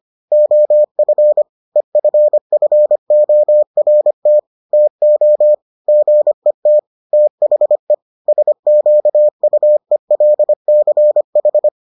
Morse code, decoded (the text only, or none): OF EFFORT TO GET THE SQUELCH